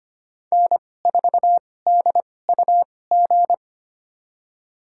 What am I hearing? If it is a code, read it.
Morse code, heard as a keyed tone, about 25 wpm, spaced as for 18 wpm, 700 Hz. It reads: N4DUG
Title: USA Call Signs - 5 Characters - 2x Character Spacing - 25wpm